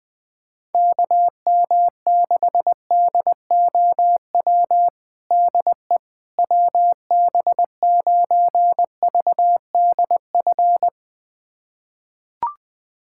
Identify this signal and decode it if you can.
Morse code, heard as a keyed tone, about 20 wpm, 700 Hz. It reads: KM6DOW DE WB9VDF
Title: USA Callsign DE Callsign - 6 Characters - 20wpm